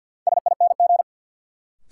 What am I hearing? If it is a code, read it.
Morse code, heard as a keyed tone, about 53 wpm, 700 Hz. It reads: SING
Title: Sentences from Top 400 Words - New Word Drill - 50wpm